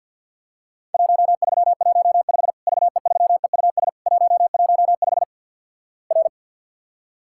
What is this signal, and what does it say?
Morse code, heard as a keyed tone, about 50 wpm, 700 Hz. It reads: 1315 VE3EUS 115 R
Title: Worked All Europe DX Contest - QTC Exchange with Repeats - 50wpm